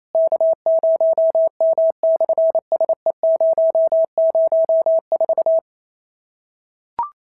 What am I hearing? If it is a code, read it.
Morse code, heard as a keyed tone, about 28 wpm, 650 Hz. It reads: K0M/SE004
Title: SOTA QSO Practice 28wpm